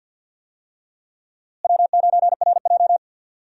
Morse code, heard as a keyed tone, about 48 wpm, 700 Hz. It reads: W9RJ
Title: USA Call Signs - 4 Characters - Repeated 3x - 50wpm